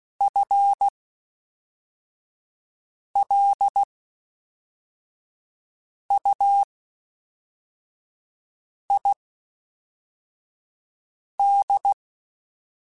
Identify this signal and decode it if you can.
Morse code, heard as a keyed tone, about 16 wpm, spaced as for 4 wpm, 800 Hz. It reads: FLUID
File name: Spaced / FL Runs sp mp3s / FL Runs sp @5